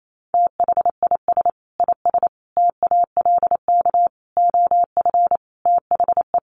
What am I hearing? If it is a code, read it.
Morse code, heard as a keyed tone, about 28 wpm, 700 Hz. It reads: THIS IS TALK OF THE
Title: Sets of 5 Words 28wpm